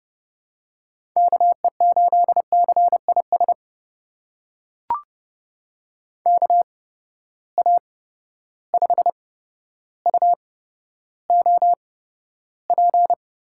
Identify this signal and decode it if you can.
Morse code, heard as a keyed tone, about 30 wpm, spaced as for 13 wpm, 700 Hz. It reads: KE8CIS KA5UOP
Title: USA Call Signs - 6 Characters - 8x Character Spacing - 30wpm